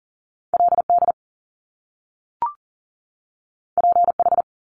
Morse code, heard as a keyed tone, about 40 wpm, 700 Hz. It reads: LD PH
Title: Top Two Letter Combinations in Words 40wpm